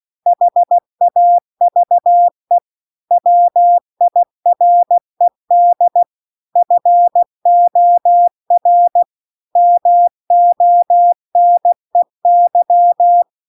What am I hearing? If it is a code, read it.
Morse code, heard as a keyed tone, about 16 wpm, 700 Hz. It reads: HAVE WIRED FOR MONEY